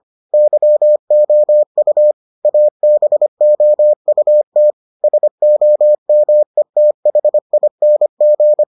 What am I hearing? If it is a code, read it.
Morse code, heard as a keyed tone, about 25 wpm, 600 Hz. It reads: YOU ABOUT SOMETHING